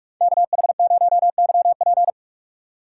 Morse code, heard as a keyed tone, about 46 wpm, 700 Hz. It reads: KH0YP
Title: USA Call Signs - 4x Character Spacing - 45wpm